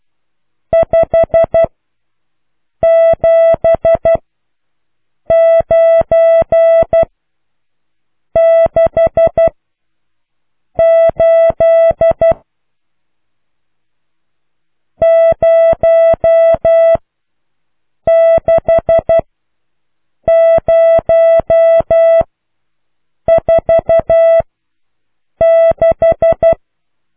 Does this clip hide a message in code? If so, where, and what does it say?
Yes: Morse code from 0.7 to 26.5 s, heard as a keyed tone, about 12 wpm, spaced as for 6 wpm, 660 Hz. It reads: 57968 06046